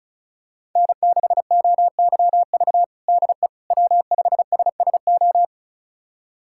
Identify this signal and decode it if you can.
Morse code, heard as a keyed tone, about 36 wpm, 700 Hz. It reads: N6OYV DE W5SSO
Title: USA Callsign DE Callsign - 5 Characters - 35wpm